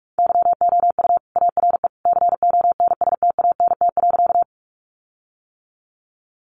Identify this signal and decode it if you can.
Morse code, heard as a keyed tone, about 44 wpm, 700 Hz. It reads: YOU ARE CONSTANT.
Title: Sentences from Top 1500 Words 45wpm